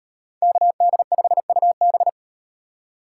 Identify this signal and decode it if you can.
Morse code, heard as a keyed tone, about 38 wpm, 700 Hz. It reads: KD5UB